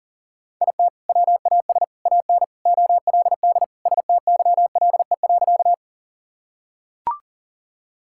Morse code, heard as a keyed tone, about 40 wpm, 700 Hz. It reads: IT WAS AN OLD STYLE.
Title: Sentences from Top 1000 Words - Review Part 5 - 40wpm